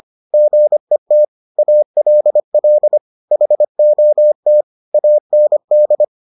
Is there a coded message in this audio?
Yes: Morse code from 0.3 to 6.0 s, heard as a keyed tone, about 25 wpm, 600 Hz. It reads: GET ALL HOT AND